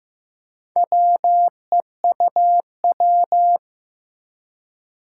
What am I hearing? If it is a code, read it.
Morse code, heard as a keyed tone, about 15 wpm, 700 Hz. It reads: WEUW
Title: Four Challenging Letter-Number Combinations 15wpm